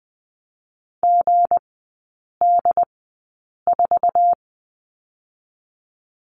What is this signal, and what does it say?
Morse code, heard as a keyed tone, about 20 wpm, spaced as for 8 wpm, 700 Hz. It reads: GD4
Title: 166 Lesson 27e - Number 7 - Three Characters 20wpm